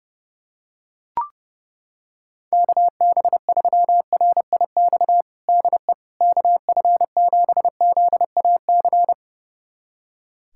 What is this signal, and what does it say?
Morse code, heard as a keyed tone, about 32 wpm, 700 Hz. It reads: KB3RIX DE KF7ZAC
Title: USA Callsign DE Callsign - 6 Characters - 30wpm